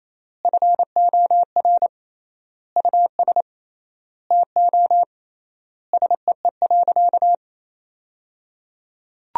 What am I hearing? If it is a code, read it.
Morse code, heard as a keyed tone, about 28 wpm, 700 Hz. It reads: FOR US TO SEE.
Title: 3x Word-Spacing - Sentences from Top 500 Words - 28wpm